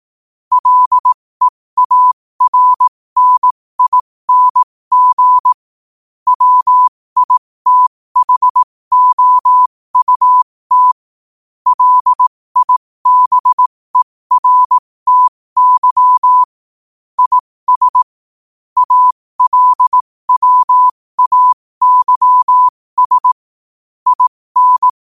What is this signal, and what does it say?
Morse code, heard as a keyed tone, about 18 wpm, spaced as for 15 wpm, 1000 Hz. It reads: LEARNING WITHOUT LIBERTY IS ALWAYS IN
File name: qotdfiles / 15wpm